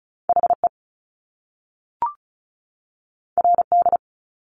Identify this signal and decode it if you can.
Morse code, heard as a keyed tone, about 36 wpm, 700 Hz. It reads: HE RD
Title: Top Two Letter Combinations in Words 35wpm